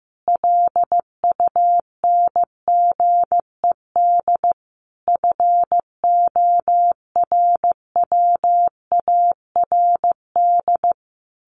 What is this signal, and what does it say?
Morse code, heard as a keyed tone, about 15 wpm, 700 Hz. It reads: LUNGED FORWARD
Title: A Sound Of Thunder 15wpm